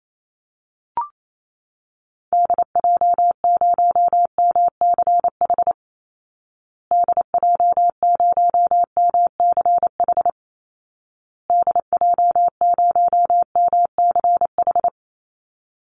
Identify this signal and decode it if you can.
Morse code, heard as a keyed tone, about 28 wpm, 700 Hz. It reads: DJ0MCH DJ0MCH DJ0MCH
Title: Call Signs Repeated 2x - 28wpm